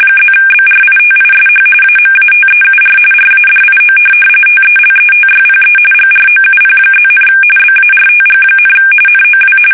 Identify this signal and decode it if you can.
RTTY, about 45.45 baud. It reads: +BRITISH AMATEUR RADIO TELETYPE GROUP+, BARTG, NOW
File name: fsk850 / teletype-RTTY